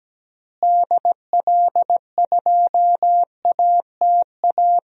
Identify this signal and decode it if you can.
Morse code, heard as a keyed tone, about 17 wpm, 700 Hz. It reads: DL2ATA
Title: DXCC-100 Call Signs - 6 Characters - Repeated 2x - 17wpm